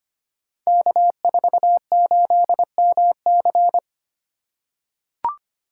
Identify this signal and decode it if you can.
Morse code, heard as a keyed tone, about 25 wpm, 700 Hz. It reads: K48MC
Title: Code Groups - 5 Letter:Numbers - 25wpm